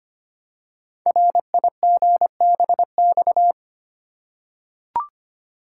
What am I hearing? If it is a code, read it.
Morse code, heard as a keyed tone, about 25 wpm, 700 Hz. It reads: RIGBX
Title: Code Groups - 5 Letters - 25wpm